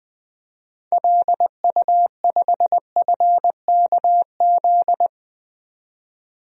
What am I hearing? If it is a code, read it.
Morse code, heard as a keyed tone, about 20 wpm, 700 Hz. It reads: LU5FKZ
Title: DXCC-100 Call Signs - 6 Characters - 20wpm